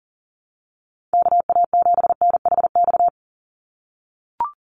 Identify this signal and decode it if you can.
Morse code, heard as a keyed tone, about 38 wpm, 700 Hz. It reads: KA7NHX
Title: USA Call Signs 40wpm